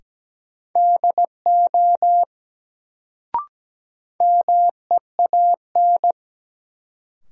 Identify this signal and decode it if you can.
Morse code, heard as a keyed tone, about 17 wpm, 700 Hz. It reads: DO MEAN
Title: Sets of 2 Words - New Word Drill - 17wpm